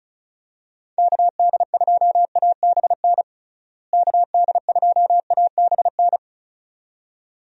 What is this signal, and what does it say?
Morse code, heard as a keyed tone, about 36 wpm, 700 Hz. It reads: KD2ABN KD2ABN
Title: USA Call Signs - 6 Characters - Repeated 2x - 35wpm